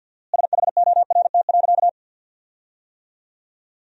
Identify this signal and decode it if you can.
Morse code, heard as a keyed tone, about 50 wpm, 700 Hz. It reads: SHORT.
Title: 4x Word-Spacing - Sentences from Top 500 Words - 50wpm